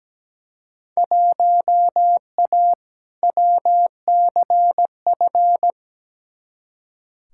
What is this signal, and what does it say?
Morse code, heard as a keyed tone, about 17 wpm, 700 Hz. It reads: 1A WCF
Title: Field Day Exchanges - 17wpm